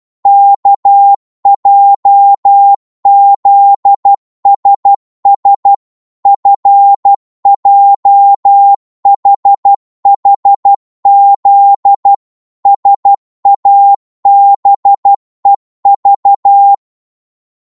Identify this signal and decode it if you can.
Morse code, heard as a keyed tone, about 12 wpm, 800 Hz. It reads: KJZSS FJHHZ SABEV
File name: Morse / ltr-12wpm